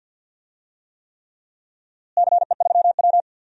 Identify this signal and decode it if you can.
Morse code, heard as a keyed tone, about 47 wpm, 700 Hz. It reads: CE3W
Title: DXCC-100 Call Signs - 4 Characters - Repeated 3x - 50wpm